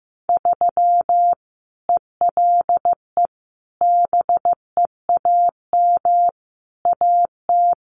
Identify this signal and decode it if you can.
Morse code, heard as a keyed tone, about 15 wpm, 700 Hz. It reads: 3 ELE BEAM AT
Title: Common QSO Phrases 15wpm